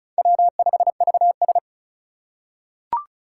Morse code, heard as a keyed tone, about 36 wpm, 690 Hz. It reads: W5VS
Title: USA Call Signs - 4 Characters - 3x Character Spacing - 35wpm